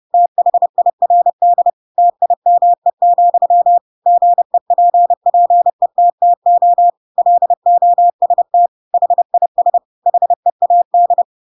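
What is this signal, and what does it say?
Morse code, heard as a keyed tone, about 30 wpm, 700 Hz. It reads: THIRD TIME, GEPPETTO LOST HIS HEAD
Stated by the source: Chapter 2 of The Adventures of Pinocchio read as a book in Morse code (CW) at 30 wpm, by ik1hge.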